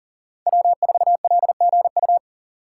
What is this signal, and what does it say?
Morse code, heard as a keyed tone, about 38 wpm, 700 Hz. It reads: W4LGU
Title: Code Groups - 5 Letter:Numbers - 40wpm